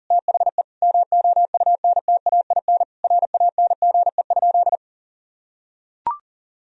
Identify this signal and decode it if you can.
Morse code, heard as a keyed tone, about 40 wpm, 700 Hz. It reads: THE MOUNTAIN RANGE?
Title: Sentences from Top 1000 Words - Review Part 4 - 40wpm